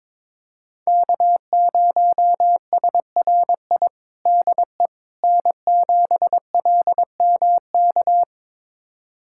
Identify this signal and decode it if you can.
Morse code, heard as a keyed tone, about 21 wpm, 700 Hz. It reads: K0SRI DE N7LMK
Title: USA Callsign DE Callsign - 5 Characters - 22wpm